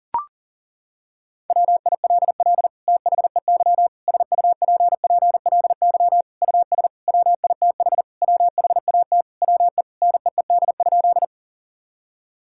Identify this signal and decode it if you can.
Morse code, heard as a keyed tone, about 40 wpm, 700 Hz. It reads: WILL THEY SUPPLY US WITH WHAT WE NEED?
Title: Sentences from Top 1000 Words - Review Part 5 - 40wpm